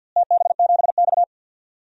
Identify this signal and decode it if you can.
Morse code, heard as a keyed tone, about 53 wpm, 700 Hz. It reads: TB7X
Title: DXCC-100 Call Signs - 4 Characters - Repeated 2x - 50wpm